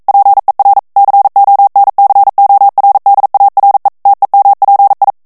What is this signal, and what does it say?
Morse code, heard as a keyed tone, about 42 wpm, 800 Hz. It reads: PER CONCORDARE TEMPI
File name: morse mp3 / ITALIEN texte clair 1 vitesse 40 mots minute